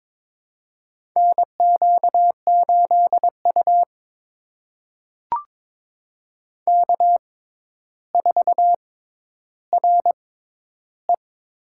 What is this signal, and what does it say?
Morse code, heard as a keyed tone, about 22 wpm, spaced as for 11 wpm, 700 Hz. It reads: NQ8U K4RE
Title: USA Call Signs - 4 Characters - 6x Character Spacing - 22wpm